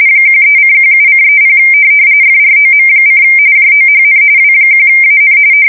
RTTY, about 45.45 baud. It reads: 13:17:30 FROM: JOHN W1AN
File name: fsk170 / USS-Missouri